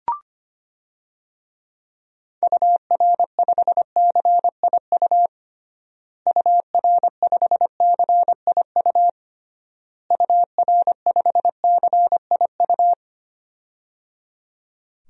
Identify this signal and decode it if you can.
Morse code, heard as a keyed tone, about 25 wpm, 700 Hz. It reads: UR5CIU UR5CIU UR5CIU
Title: DXCC-100 Call Signs - 6 Characters - Repeated 3x - 25wpm